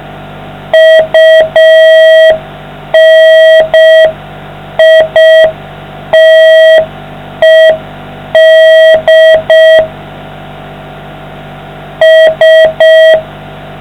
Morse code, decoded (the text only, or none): UNITED S